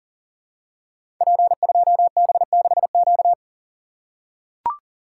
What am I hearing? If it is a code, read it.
Morse code, heard as a keyed tone, about 38 wpm, 700 Hz. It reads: P2B6Q
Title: Code Groups - 5 Letter:Numbers - 40wpm